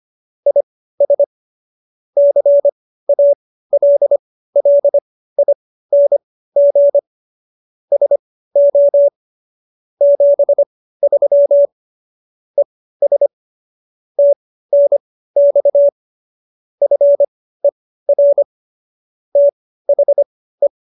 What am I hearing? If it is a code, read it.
Morse code, heard as a keyed tone, about 25 wpm, spaced as for 15 wpm, 570 Hz. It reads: IS CALLING SO 73 ES TNX FER THE
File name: qso222 15